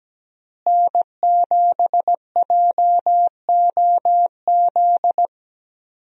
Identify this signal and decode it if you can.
Morse code, heard as a keyed tone, about 17 wpm, 700 Hz. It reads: N7JOZ
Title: USA Call Signs - 5 Characters - Repeated 2x - 17wpm